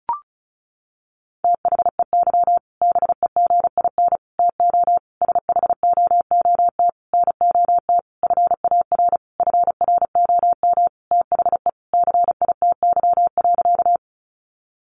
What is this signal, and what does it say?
Morse code, heard as a keyed tone, about 34 wpm, 700 Hz. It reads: THEY BEGIN TO SHOOT NOT FAR FROM THE CITY.
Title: Sentences from Top 1200 Words 35wpm